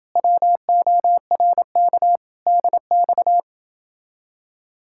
Morse code, heard as a keyed tone, about 29 wpm, 700 Hz. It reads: WORK DX